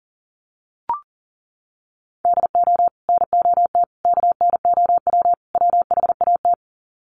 Morse code, heard as a keyed tone, about 40 wpm, 700 Hz. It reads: DO NOT KNOW WHAT
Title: Sets of 4 Words 40wpm